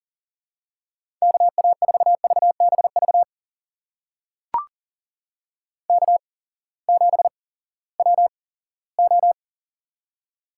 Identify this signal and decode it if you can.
Morse code, heard as a keyed tone, about 39 wpm, spaced as for 20 wpm, 700 Hz. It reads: KA4VBV K7WO
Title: USA Call Signs - 8x Character Spacing - 40wpm